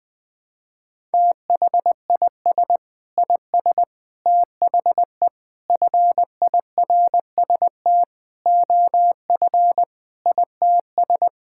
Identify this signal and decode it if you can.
Morse code, heard as a keyed tone, about 21 wpm, 700 Hz. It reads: THIS IS THE FIRST OF ITS
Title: Sentences with 6-Word Phrases - 20wpm